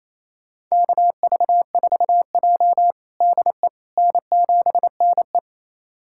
Morse code, heard as a keyed tone, about 28 wpm, 700 Hz. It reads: KV4J DE N7NE
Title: USA Callsign DE Callsign - 4 Characters - 28wpm